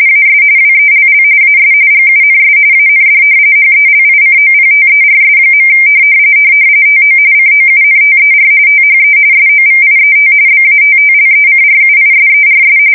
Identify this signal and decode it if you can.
RTTY, about 45.45 baud. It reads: AS WELL AS GUNS AND HELICOPTERS. TO COUNTER THE SOVIET BATTLECRUISERS
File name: fsk170 / battleshipnj